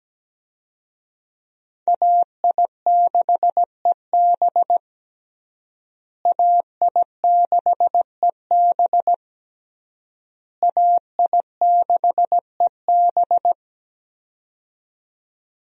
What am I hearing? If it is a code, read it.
Morse code, heard as a keyed tone, about 17 wpm, 700 Hz. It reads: AI6EB AI6EB AI6EB
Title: USA Call Signs - 5 Characters - Repeated 3x - 17wpm